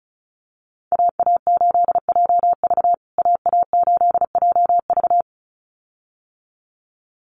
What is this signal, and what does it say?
Morse code, heard as a keyed tone, about 34 wpm, 700 Hz. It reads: AA8JV AA8JV
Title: USA Call Signs Repeated 2x 35wpm